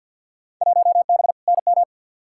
Morse code, heard as a keyed tone, about 53 wpm, 700 Hz. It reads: 1B NM
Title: Field Day Exchanges Encore - 50wpm